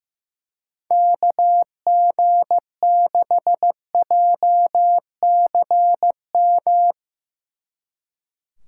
Morse code, heard as a keyed tone, about 16 wpm, 700 Hz. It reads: KG6JCM
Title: Speed Racing - USA 2x3 Call Signs 15wpm